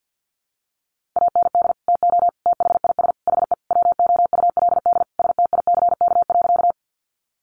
Morse code, heard as a keyed tone, about 50 wpm, 700 Hz. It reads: AND TO THIS HE WOULD STICK.
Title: Sentences from Top 1200 Words 50wpm